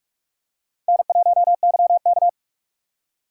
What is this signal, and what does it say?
Morse code, heard as a keyed tone, about 46 wpm, 690 Hz. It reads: N1YK
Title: USA Call Signs - 4 Characters - 3x Character Spacing - 45wpm